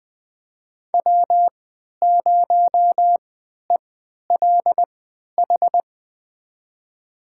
Morse code, heard as a keyed tone, about 20 wpm, spaced as for 11 wpm, 700 Hz. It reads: W0ELH
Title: USA Call Signs - 5 Characters - 3x Character Spacing - 20wpm